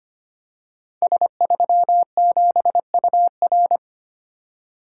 Morse code, heard as a keyed tone, about 25 wpm, 700 Hz. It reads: S37UR
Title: Code Groups - 5 Letter:Numbers - 25wpm